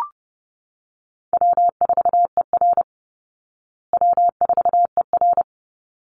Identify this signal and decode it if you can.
Morse code, heard as a keyed tone, about 30 wpm, 700 Hz. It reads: W4ER W4ER
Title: Call Signs Repeated 3x - 30wpm